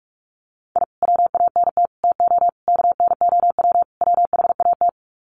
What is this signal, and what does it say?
Morse code, heard as a keyed tone, about 44 wpm, 700 Hz. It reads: I WANT TO KNOW WHAT
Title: Sets of 5 Words Encore2 45wpm